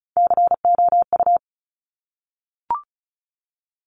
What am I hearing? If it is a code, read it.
Morse code, heard as a keyed tone, about 36 wpm, 700 Hz. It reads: COU
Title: Top Three Letter Combinations in Words 35wpm